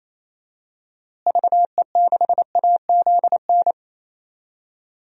Morse code, heard as a keyed tone, about 28 wpm, 700 Hz. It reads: VE6AZN
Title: DXCC-100 Call Signs - 6 Characters - Repeated 3x - 28wpm